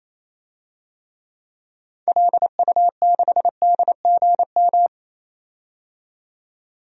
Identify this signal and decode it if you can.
Morse code, heard as a keyed tone, about 28 wpm, 700 Hz. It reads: LU6DGM
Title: DXCC-100 Call Signs - 6 Characters - 28wpm